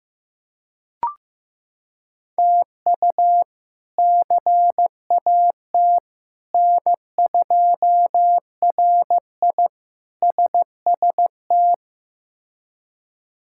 Morse code, heard as a keyed tone, about 15 wpm, 700 Hz. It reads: TU CAT N2RI SST
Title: K1USN SST QSO Practice - 15wpm